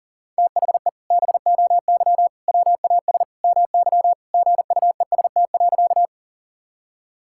Morse code, heard as a keyed tone, about 37 wpm, 700 Hz. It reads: THE BOY WAS MY GUEST.
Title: Sentences from Top 1600 Words 40wpm